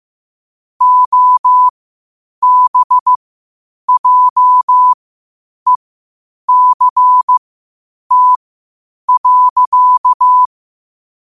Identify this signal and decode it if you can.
Morse code, heard as a keyed tone, about 15 wpm, spaced as for 8 wpm, 1000 Hz. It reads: OBJECT.
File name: qotdfiles / 8wpm